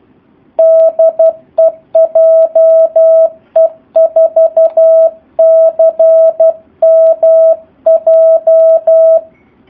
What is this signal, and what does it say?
Morse code, heard as a keyed tone, about 12 wpm, 650 Hz. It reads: DE JE4CMJ
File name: jh8cbh / 2019cw / SP04- w